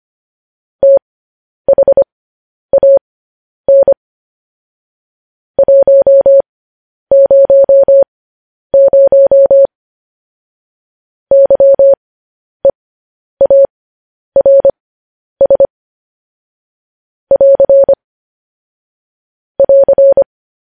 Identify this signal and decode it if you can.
Morse code, heard as a keyed tone, about 25 wpm, spaced as for 10 wpm, 550 Hz. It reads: THAN 100 YEARS <AR> <AR>
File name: podcast / News headlines-1x-25x10